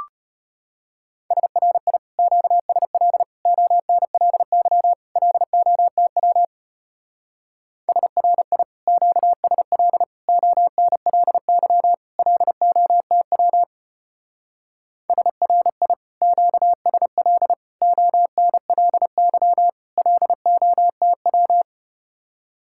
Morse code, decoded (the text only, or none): SRI QSL ONLY LOTW SRI QSL ONLY LOTW SRI QSL ONLY LOTW